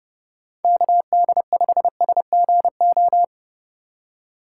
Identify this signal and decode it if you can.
Morse code, heard as a keyed tone, about 32 wpm, 700 Hz. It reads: KD5SGO
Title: USA Call Signs - 6 Characters - 2x Character Spacing - 30wpm